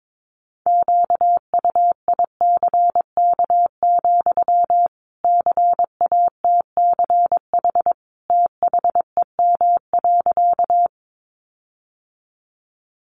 Morse code, heard as a keyed tone, about 22 wpm, 700 Hz. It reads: QUICK, CATCH THEM.
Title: Sentences from Top 1500 Words 22wpm